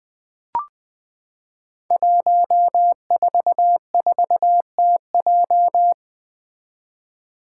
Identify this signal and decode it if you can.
Morse code, heard as a keyed tone, about 20 wpm, 700 Hz. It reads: 144TJ
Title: Five Challenging Letter-Number Combinations 20wpm